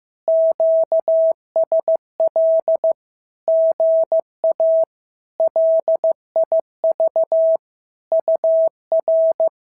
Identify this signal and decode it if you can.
Morse code, heard as a keyed tone, about 15 wpm, 650 Hz. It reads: QSL GA LIV UR
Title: SOTA QSO Practice 15wpm